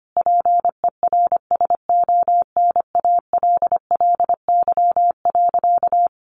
Morse code, heard as a keyed tone, about 25 wpm, 700 Hz. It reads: PERSONALLY.
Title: Sentences from Top 1400 Words 25wpm